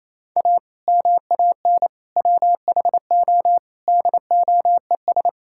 Morse code, heard as a keyed tone, about 28 wpm, 700 Hz. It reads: A MAN WHO DOES
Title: Sentences from Top 1600 Words 28wpm